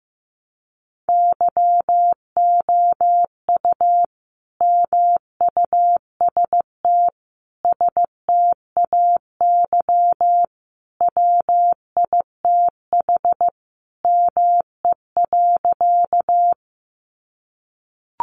Morse code, heard as a keyed tone, about 15 wpm, 700 Hz. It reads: YOU MUST STAY WITH ME.
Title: Sentences from Top 500 Words 15wpm